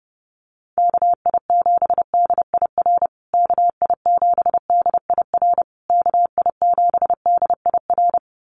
Morse code, heard as a keyed tone, about 30 wpm, 700 Hz. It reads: KI7DIR KI7DIR KI7DIR
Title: USA Call Signs Repeated 3x 30wpm